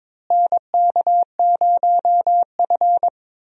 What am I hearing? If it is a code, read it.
Morse code, heard as a keyed tone, about 21 wpm, 700 Hz. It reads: NK0F
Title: USA Call Signs - 4 Characters - Repeated 3x - 22wpm